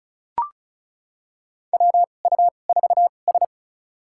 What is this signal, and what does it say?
Morse code, heard as a keyed tone, about 34 wpm, spaced as for 25 wpm, 700 Hz. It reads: WU4S
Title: USA Call Signs - 4 Characters - 2x Character Spacing - 35wpm